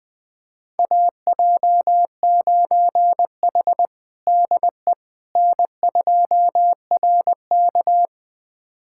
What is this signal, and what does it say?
Morse code, heard as a keyed tone, about 21 wpm, 700 Hz. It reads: AJ9H DE N2RK
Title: USA Callsign DE Callsign - 4 Characters - 20wpm